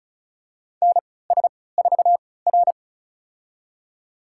Morse code, heard as a keyed tone, about 36 wpm, spaced as for 20 wpm, 700 Hz. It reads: NS4R
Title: USA Call Signs - 4 Characters - 3x Character Spacing - 35wpm